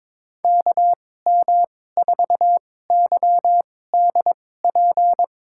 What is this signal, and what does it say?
Morse code, heard as a keyed tone, about 21 wpm, spaced as for 16 wpm, 700 Hz. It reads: KM4YDP
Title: USA Call Signs - 2x Character Spacing - 22wpm